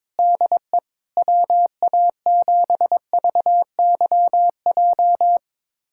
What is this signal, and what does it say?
Morse code, heard as a keyed tone, about 22 wpm, 700 Hz. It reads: DE WA7VYJ
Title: USA Callsign DE Callsign 22wpm